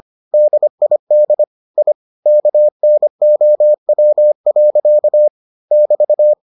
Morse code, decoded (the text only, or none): DID I KNOW. <BT>